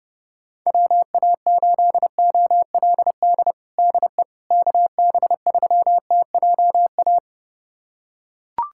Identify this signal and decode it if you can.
Morse code, heard as a keyed tone, about 30 wpm, 700 Hz. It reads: WA8OLD DE KB3TJA
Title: USA Callsign DE Callsign - 6 Characters - 30wpm